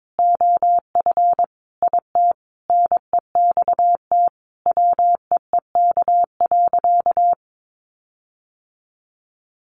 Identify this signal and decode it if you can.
Morse code, heard as a keyed tone, about 22 wpm, 700 Hz. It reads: OF IT NEXT WEEK.
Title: Sentences from Top 1400 Words 22wpm